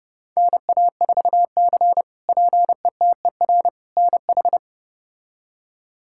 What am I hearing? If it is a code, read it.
Morse code, heard as a keyed tone, about 30 wpm, 700 Hz. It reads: NA4C PETER NH
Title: K1USN SST QSO Practice - 30wpm